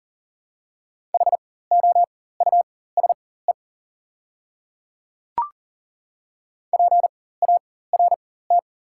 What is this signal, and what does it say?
Morse code, heard as a keyed tone, about 40 wpm, spaced as for 19 wpm, 700 Hz. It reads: HOUSE PART
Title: 4x Character Spacing - Top 200 Words - 40wpm